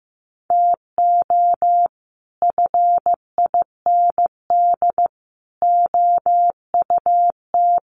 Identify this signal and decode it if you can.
Morse code, heard as a keyed tone, about 15 wpm, 700 Hz. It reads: TO FIND OUT
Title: Sets of 3 Words 15wpm